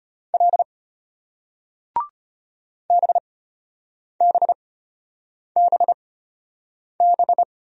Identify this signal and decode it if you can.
Morse code, heard as a keyed tone, about 33 wpm, spaced as for 8 wpm, 700 Hz. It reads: L BBBB